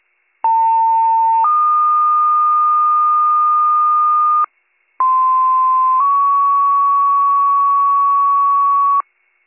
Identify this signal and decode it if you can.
Two-tone page, heard as a one-second tone, then a three-second tone, about 900 then 1220 Hz, about 1040 then 1120 Hz. It is heard from 0.4 to 9.0 s.